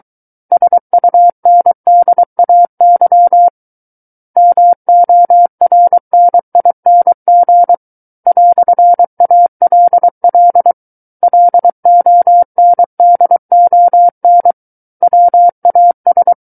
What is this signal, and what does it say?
Morse code, heard as a keyed tone, about 23 wpm, 700 Hz. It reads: SUNDAY MORNING "ALL LONDON WAS
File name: Book One / 14-Chapter 14 - 23 WPM